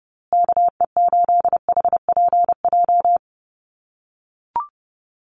Morse code, heard as a keyed tone, about 29 wpm, 700 Hz. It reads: KE8HPJ
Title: USA Call Signs Repeated 3x 30wpm